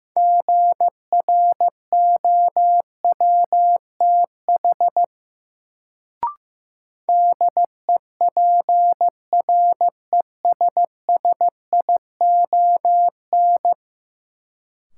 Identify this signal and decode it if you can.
Morse code, heard as a keyed tone, about 15 wpm, 700 Hz. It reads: GROWTH DEPRESSION
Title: Sentences from Top 1300 Words - New Word Drill Alt - 15wpm